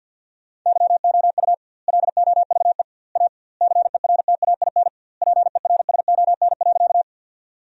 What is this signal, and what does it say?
Morse code, heard as a keyed tone, about 51 wpm, 700 Hz. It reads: YOU LOVE A CERTAIN PERSON.
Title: 2x Word-Spacing - Sentences from Top 500 Words - 50wpm